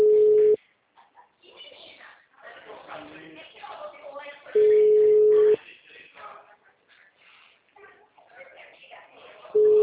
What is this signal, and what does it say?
Ringback tone (Europe, 425 Hz), with speech.
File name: out- -2007- - - .25872